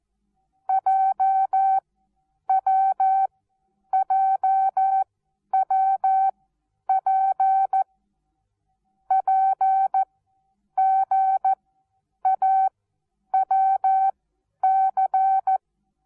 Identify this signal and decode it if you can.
Morse code, heard as a keyed tone, about 14 wpm, spaced as for 9 wpm, 770 Hz. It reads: JWJWP PGAWC
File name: Onlinelehrgang / morsen / DJ4UF Morsen19